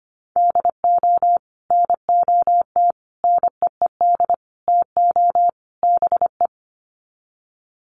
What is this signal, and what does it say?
Morse code, heard as a keyed tone, about 25 wpm, 700 Hz. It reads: DO NOT NEED TO BE
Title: Sets of 5 Words Encore 25wpm